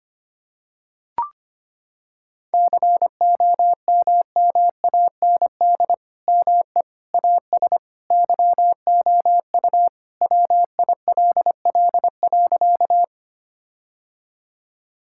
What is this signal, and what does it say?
Morse code, heard as a keyed tone, about 25 wpm, 700 Hz. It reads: COMMAND ME AS YOU WILL.
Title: Sentences from Top 1600 Words 25wpm